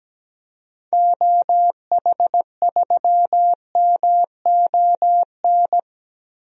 Morse code, heard as a keyed tone, about 17 wpm, 700 Hz. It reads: OH3MON
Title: DXCC-100 Call Signs - 6 Characters - 17wpm